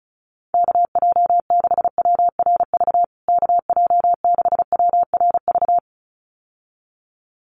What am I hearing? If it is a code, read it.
Morse code, heard as a keyed tone, about 34 wpm, 700 Hz. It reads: KJ6WRV KJ6WRV
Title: USA Call Signs Repeated 2x 35wpm